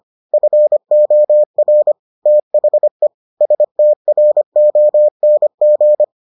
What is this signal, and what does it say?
Morse code, heard as a keyed tone, about 25 wpm, 600 Hz. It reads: FOR THE STRONG